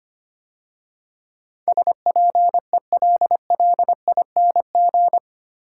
Morse code, heard as a keyed tone, about 25 wpm, 690 Hz. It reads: SPELLING
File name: 8 Letter Words - First 3 Letters Given - 25wpm